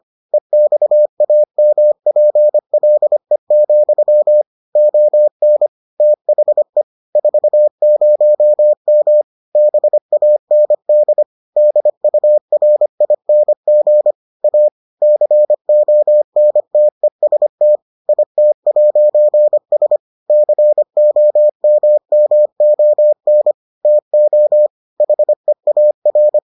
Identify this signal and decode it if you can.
Morse code, heard as a keyed tone, about 25 wpm, 600 Hz. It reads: EXAMPLE, ON THE 40M BAND DURING A CONTEST IT'S COMMON TO HEAR